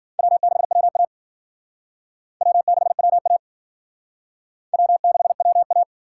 Morse code, heard as a keyed tone, about 51 wpm, 700 Hz. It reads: W6WA W6WA W6WA